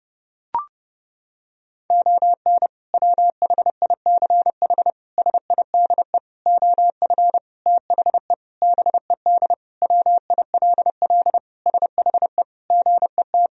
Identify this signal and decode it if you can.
Morse code, heard as a keyed tone, about 30 wpm, 700 Hz. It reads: ON WHICH SIDE OF THE BED WILL SHE GET
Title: Sentences from Top 1000 Words - Review Part 1 - 30wpm